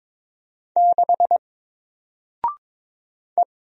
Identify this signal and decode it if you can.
Morse code, heard as a keyed tone, about 22 wpm, 700 Hz. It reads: 6 E
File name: Single-Letter-Number-22wpm